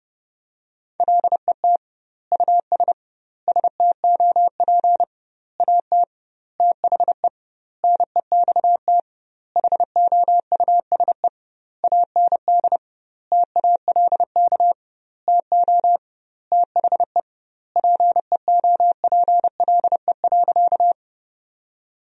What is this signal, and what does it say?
Morse code, heard as a keyed tone, about 30 wpm, 700 Hz. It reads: LET US STOP AT THE NEXT HOUSE AND TALK TO THE PEOPLE.
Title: 2x Word-Spacing - Sentences from Top 400 Words - 30wpm